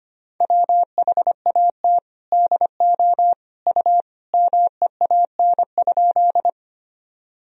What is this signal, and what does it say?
Morse code, heard as a keyed tone, about 25 wpm, 700 Hz. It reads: WHAT DO U MEAN?
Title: Sentences from Top 200 Words and Abbreviations 25wpm